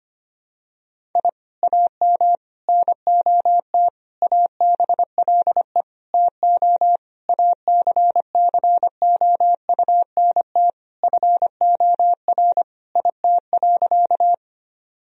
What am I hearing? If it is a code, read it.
Morse code, heard as a keyed tone, about 25 wpm, 700 Hz. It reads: I AM NOT ABLE TO ACCOUNT FOR IT.
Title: Sentences from Top 1000 Words - Review Part 2 - 25wpm